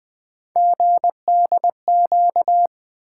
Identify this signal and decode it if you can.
Morse code, heard as a keyed tone, about 20 wpm, 700 Hz. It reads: GDQ
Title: Three Letter-Number Combinations - 20wpm